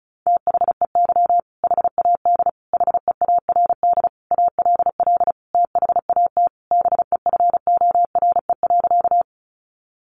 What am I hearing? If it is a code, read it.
Morse code, heard as a keyed tone, about 34 wpm, 700 Hz. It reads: THEY HAD HEARD ALL THAT BEFORE.
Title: Sentences from Top 400 Words 35wpm